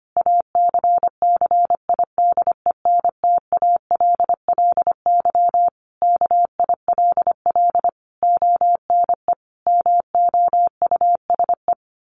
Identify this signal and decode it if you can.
Morse code, heard as a keyed tone, about 25 wpm, 700 Hz. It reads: ACCIDENTALLY KILL ONE MOUSE
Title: A Sound Of Thunder 25wpm